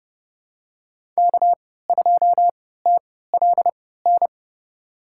Morse code, heard as a keyed tone, about 31 wpm, spaced as for 17 wpm, 700 Hz. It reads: K2TLN
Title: USA Call Signs - 5 Characters - 3x Character Spacing - 30wpm